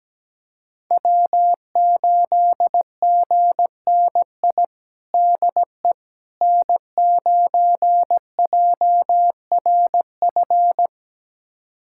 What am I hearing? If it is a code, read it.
Morse code, heard as a keyed tone, about 17 wpm, 700 Hz. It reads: W8GNI DE N9JRF
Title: USA Callsign DE Callsign - 5 Characters - 17wpm